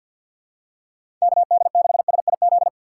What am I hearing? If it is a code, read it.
Morse code, heard as a keyed tone, about 47 wpm, 700 Hz. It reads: KD6SIZ
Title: USA Call Signs - 6 Characters - 4x Character Spacing - 50wpm